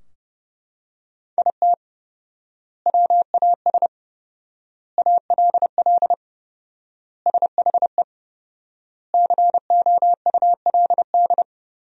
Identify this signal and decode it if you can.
Morse code, heard as a keyed tone, about 31 wpm, 700 Hz. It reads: IT WAS ALL SHE COULD
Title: 4x Word-Spacing - Sentences from Top 400 Words - 30wpm